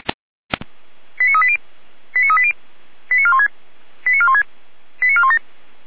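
Five-tone selective call (ZVEI), heard as quick runs of five tones, tones of about 70 ms. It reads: 89380 89380 89416 89416 89417
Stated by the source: ZVEI five-tone selective call received on an SDR.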